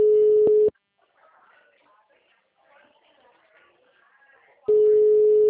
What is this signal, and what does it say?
Ringback tone (Europe, 425 Hz), with speech.